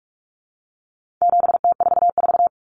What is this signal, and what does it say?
Morse code, heard as a keyed tone, about 44 wpm, 700 Hz. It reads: 7T44
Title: Common Cut Numbers - 4-Digit 45wpm